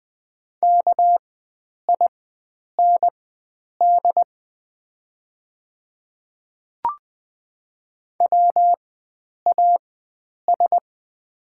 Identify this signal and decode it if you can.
Morse code, heard as a keyed tone, about 21 wpm, spaced as for 9 wpm, 700 Hz. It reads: KIND WAS
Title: 4x Character Spacing - Top 300 Words - 20wpm